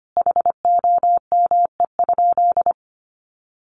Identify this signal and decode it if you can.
Morse code, heard as a keyed tone, about 25 wpm, 700 Hz. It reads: HOME?
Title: Sentences from Top 1200 Words 25wpm